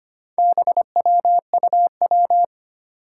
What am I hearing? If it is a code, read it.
Morse code, heard as a keyed tone, about 25 wpm, 700 Hz. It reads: BWUW
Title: Four Challenging Letter-Number Combinations 25wpm